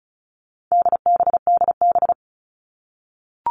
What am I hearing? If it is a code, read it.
Morse code, heard as a keyed tone, about 34 wpm, 700 Hz. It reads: DBDB
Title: All Cut Numbers - 4-Digit 35wpm